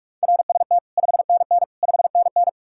Morse code, heard as a keyed tone, about 56 wpm, 700 Hz. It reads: RST 5NN 5NN
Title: Speed-Racing Common QSO Phrases - 45wpm